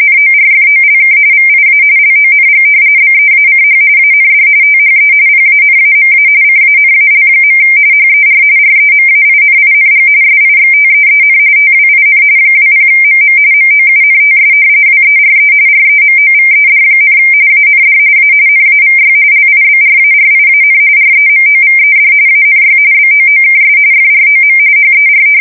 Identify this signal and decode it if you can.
RTTY, about 45.45 baud. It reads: NORFOLK VIA LONG BEACH AND THE PANAMA CANAL. HER REMAINING SIXTEEN TWIN 20MM MOUNTS WERE REMOVED, AND A NAVY LIBRARIAN REMOVED ANY BOOKS FROM HER
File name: fsk170 / battleshipnj